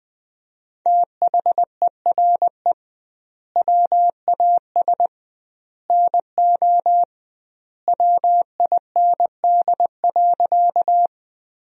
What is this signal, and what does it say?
Morse code, heard as a keyed tone, about 20 wpm, 700 Hz. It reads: THERE WAS NO WIND.
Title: 2x Word-Spacing - Sentences from Top 400 Words - 20wpm